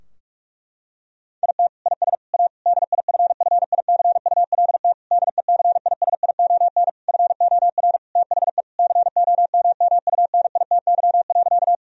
Morse code, heard as a keyed tone, about 46 wpm, 700 Hz. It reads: IT IS A DIFFICULT DECISION FOR THE COMMUNITY.
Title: Sentences from Top 1000 Words - Review Part 4 - 45wpm